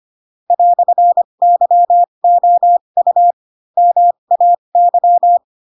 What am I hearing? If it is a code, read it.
Morse code, heard as a keyed tone, about 25 wpm, spaced as for 22 wpm, 700 Hz. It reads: "YOU MAY
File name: Pinocchio / Pinocchio ch7 18 25 ik1hge